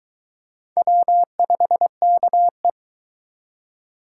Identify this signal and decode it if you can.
Morse code, heard as a keyed tone, about 23 wpm, 700 Hz. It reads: W5KE